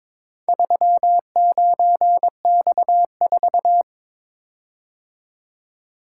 Morse code, heard as a keyed tone, about 22 wpm, 700 Hz. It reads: 39X4
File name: practice / 22wpm / Four Letter-Number Combinations - 22wpm